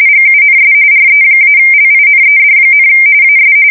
RTTY, about 45.45 baud. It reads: GUNS RE-LINED, AS
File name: fsk170 / USS-Massachusetts